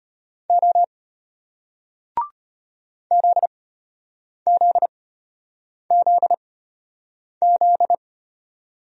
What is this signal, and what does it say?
Morse code, heard as a keyed tone, about 32 wpm, spaced as for 8 wpm, 700 Hz. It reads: O ZZZZ